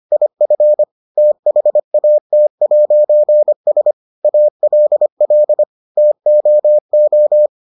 Morse code, heard as a keyed tone, about 25 wpm, 600 Hz. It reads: IF THAT'S ALL TOO